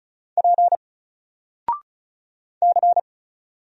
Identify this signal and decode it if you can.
Morse code, heard as a keyed tone, about 36 wpm, 700 Hz. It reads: P C
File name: Single-Letter-Number-35wpm